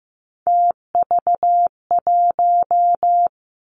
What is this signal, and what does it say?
Morse code, heard as a keyed tone, about 15 wpm, 700 Hz. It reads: TV1
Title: Three Challenging Letter-Number Combinations 15wpm